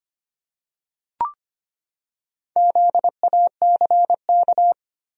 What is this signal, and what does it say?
Morse code, heard as a keyed tone, about 25 wpm, 700 Hz. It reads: ZACK
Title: USA Common Names - 25wpm